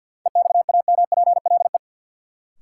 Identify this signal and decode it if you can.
Morse code, heard as a keyed tone, about 47 wpm, 700 Hz. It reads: EXAMPLE
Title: Sets of 2 Words - New Word Drill - 50wpm